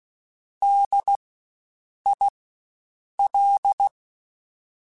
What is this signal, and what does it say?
Morse code, heard as a keyed tone, about 16 wpm, spaced as for 7 wpm, 800 Hz. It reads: DIL
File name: Spaced Alphas / Spaced Alphas mp3s / Ds sp @15